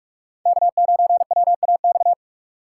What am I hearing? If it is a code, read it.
Morse code, heard as a keyed tone, about 42 wpm, 700 Hz. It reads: K9WAX